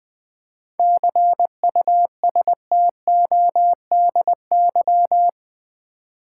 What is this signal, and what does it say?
Morse code, heard as a keyed tone, about 20 wpm, 700 Hz. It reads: CUSTODY
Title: 7 Letter Words - First 3 Letters Given - 20wpm